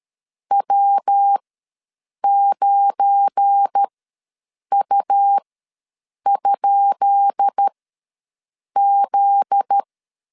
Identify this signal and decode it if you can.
Morse code, heard as a keyed tone, about 13 wpm, spaced as for 7 wpm, 800 Hz. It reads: W9U?Z
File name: LEZIONI / 10 SIGNS / 10B INT ws 12wpm 3527